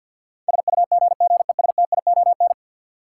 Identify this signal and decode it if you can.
Morse code, heard as a keyed tone, about 53 wpm, 700 Hz. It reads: SUGGESTION
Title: Sentences from Top 1600 Words - New Word Drill Alt - 50wpm